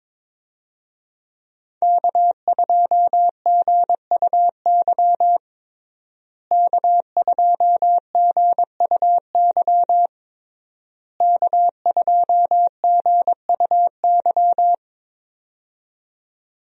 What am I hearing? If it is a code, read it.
Morse code, heard as a keyed tone, about 22 wpm, 700 Hz. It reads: K2GUY K2GUY K2GUY
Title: USA Call Signs - 5 Characters - Repeated 3x - 22wpm